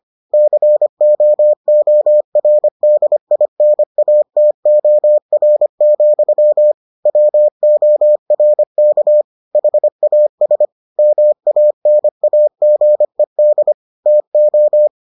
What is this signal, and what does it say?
Morse code, heard as a keyed tone, about 25 wpm, 600 Hz. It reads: COORDINATOR, WORK HAS MANAGED TO